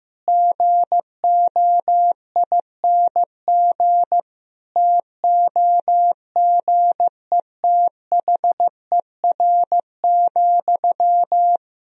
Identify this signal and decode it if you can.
Morse code, heard as a keyed tone, about 16 wpm, 700 Hz. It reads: GOING TOGETHER,
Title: Sentences from Top 1000 Words - Review Part 4 - 15wpm